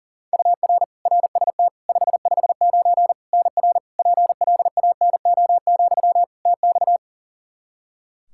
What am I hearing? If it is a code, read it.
Morse code, heard as a keyed tone, about 40 wpm, 700 Hz. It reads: UR RST 559 NR PLANO, TX
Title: Practice QSO - Exchange 1 - Beginning - 40wpm